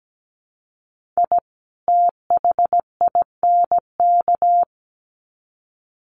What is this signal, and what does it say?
Morse code, heard as a keyed tone, about 17 wpm, 700 Hz. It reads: I THINK
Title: Sets of 2 Words 17wpm